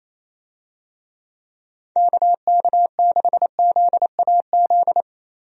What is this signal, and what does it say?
Morse code, heard as a keyed tone, about 28 wpm, 700 Hz. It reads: KK6ZAZ
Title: USA Call Signs - 6 Characters - 28wpm